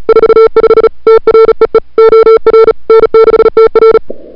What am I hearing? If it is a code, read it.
Morse code, heard as a keyed tone, about 36 wpm, 440 Hz. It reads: 45 TREE OR N6TR
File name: CWSprint / February 2007 CW Sprint / 01-2007 CW Sprint Hour1